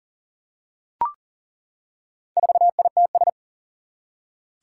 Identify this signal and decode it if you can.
Morse code, heard as a keyed tone, about 40 wpm, 700 Hz. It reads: 4ITS
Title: Four Challenging Letter-Number Combinations 40wpm